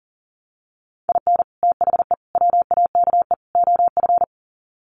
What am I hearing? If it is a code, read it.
Morse code, heard as a keyed tone, about 40 wpm, 700 Hz. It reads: IN THE WAKE OF
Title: Sets of 4 Words 40wpm